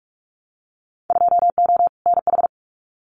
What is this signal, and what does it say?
Morse code, heard as a keyed tone, about 46 wpm, 690 Hz. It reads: 2O NH
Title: Winter Field Day Exchanges Encore 45wpm